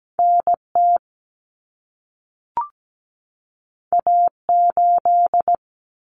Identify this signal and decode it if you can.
Morse code, heard as a keyed tone, about 17 wpm, 700 Hz. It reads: NT A8
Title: Common Cut Numbers - 2-Digit 17wpm